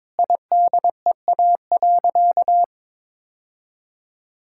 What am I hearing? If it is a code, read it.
Morse code, heard as a keyed tone, about 22 wpm, 700 Hz. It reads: IDEA.
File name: practice / 22wpm / Sentences with 3-Word Phrases - 22wpm